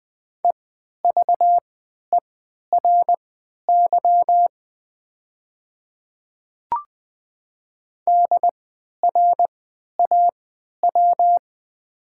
Morse code, heard as a keyed tone, about 20 wpm, spaced as for 11 wpm, 700 Hz. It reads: EVERY DRAW
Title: 3x Character Spacing - Top 300 Words - 20wpm